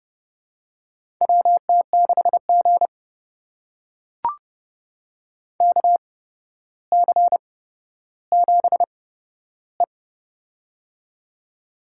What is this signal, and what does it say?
Morse code, heard as a keyed tone, about 30 wpm, spaced as for 13 wpm, 700 Hz. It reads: WT6G KC7E
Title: USA Call Signs - 4 Characters - 8x Character Spacing - 30wpm